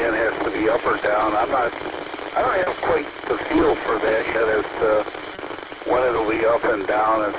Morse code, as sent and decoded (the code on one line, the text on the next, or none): none